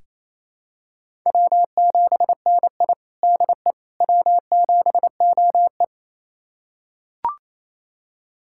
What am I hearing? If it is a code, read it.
Morse code, heard as a keyed tone, about 28 wpm, 700 Hz. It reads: W7NI DE W7OE
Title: USA Callsign DE Callsign - 4 Characters - 28wpm